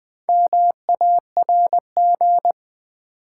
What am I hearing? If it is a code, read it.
Morse code, heard as a keyed tone, about 20 wpm, 700 Hz. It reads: MARG
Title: Build A Word - 2 Letters At A Time - 8-Letter Words - 20wpm